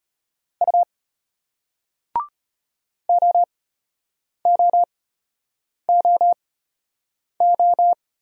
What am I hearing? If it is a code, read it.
Morse code, heard as a keyed tone, about 34 wpm, spaced as for 8 wpm, 700 Hz. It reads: U OOOO